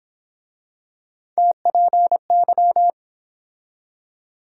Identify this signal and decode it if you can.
Morse code, heard as a keyed tone, about 30 wpm, 700 Hz. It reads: TPY